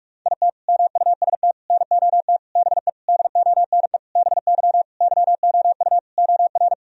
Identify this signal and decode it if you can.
Morse code, heard as a keyed tone, about 46 wpm, 700 Hz. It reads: IT MUST NOT BE DONE BY YOU OR
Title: Sentences from Top 1000 Words - Review Part 1 - 45wpm